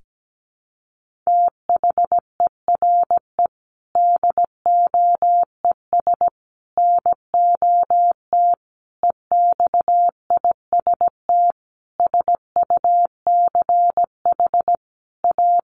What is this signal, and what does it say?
Morse code, heard as a keyed tone, about 17 wpm, 700 Hz. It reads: THERE DOES NOT EXIST SUCH A
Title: Sentences from Top 1400 Words 17wpm